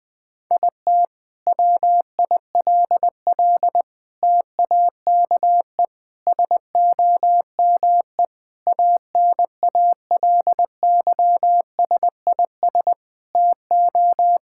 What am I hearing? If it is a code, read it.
Morse code, heard as a keyed tone, about 21 wpm, 700 Hz. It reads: IT WILL TAKE SOME ANALYSIS TO
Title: Sentences from Top 1000 Words - Review Part 5 - 20wpm